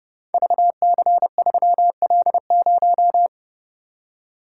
Morse code, heard as a keyed tone, about 30 wpm, 700 Hz. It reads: VC3L0
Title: Code Groups - 5 Letter:Numbers - 30wpm